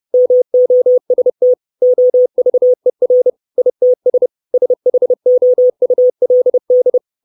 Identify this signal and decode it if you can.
Morse code, heard as a keyed tone, about 30 wpm, 500 Hz. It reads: MOST OVER ITS SHOULD
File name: 632 Words 30WPM